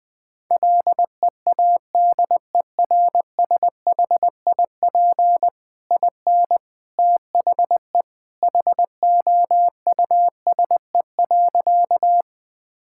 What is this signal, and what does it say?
Morse code, heard as a keyed tone, about 20 wpm, 700 Hz. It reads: LEADERSHIP IN THE HOUSE.
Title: Sentences from Top 1600 Words 20wpm